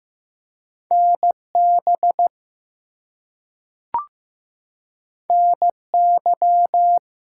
Morse code, heard as a keyed tone, about 15 wpm, 700 Hz. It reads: NB NY
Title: US States and Canadian Provinces 15wpm